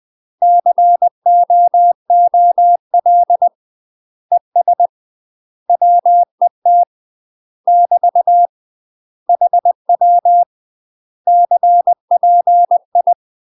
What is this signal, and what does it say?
Morse code, heard as a keyed tone, about 20 wpm, 700 Hz. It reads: COOL ES WET <BT> HW CPI
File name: UK MASTER QSOs TRAINING / 20WPM / UK MASTER QSOs 0016